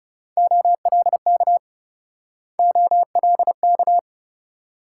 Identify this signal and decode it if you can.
Morse code, heard as a keyed tone, about 38 wpm, 700 Hz. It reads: OLK OLK